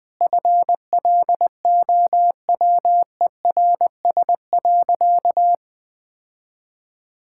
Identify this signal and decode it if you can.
Morse code, heard as a keyed tone, about 21 wpm, 700 Hz. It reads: FLOWERS.
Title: Sentences with 9-Word Phrases - 20wpm